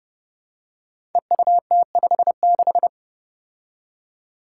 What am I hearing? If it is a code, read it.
Morse code, heard as a keyed tone, about 30 wpm, 700 Hz. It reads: EUT56
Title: Code Groups - 5 Letter:Numbers - 30wpm